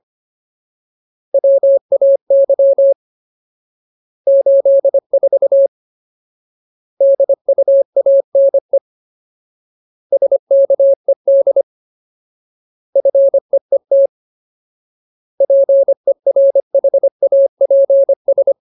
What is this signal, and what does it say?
Morse code, heard as a keyed tone, about 25 wpm, 550 Hz. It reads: WAY 84 DUANE SKED FEET PERHAPS
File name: Ultimate Word List / 25WPM Ultimate